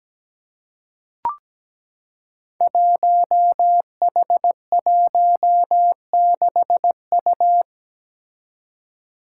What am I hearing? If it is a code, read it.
Morse code, heard as a keyed tone, about 17 wpm, 700 Hz. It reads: 1H16U
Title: Five Challenging Letter-Number Combinations 17wpm